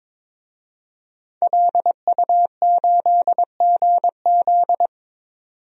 Morse code, heard as a keyed tone, about 22 wpm, 700 Hz. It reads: LU8GZ
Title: DXCC-100 Call Signs - 5 Characters - 22wpm